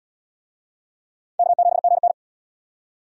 Morse code, heard as a keyed tone, about 66 wpm, 700 Hz. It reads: K7GN